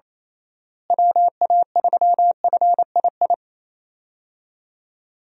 Morse code, heard as a keyed tone, about 28 wpm, 700 Hz. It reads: WA3FII
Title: USA Call Signs - 6 Characters - Repeated 2x - 28wpm